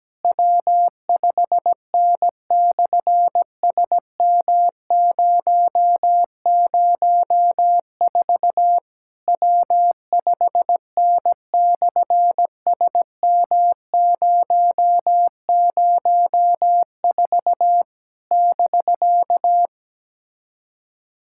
Morse code, heard as a keyed tone, about 17 wpm, 700 Hz. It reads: W5N/SM004 W5N/SM004 <BK>
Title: SOTA QSO Practice 17wpm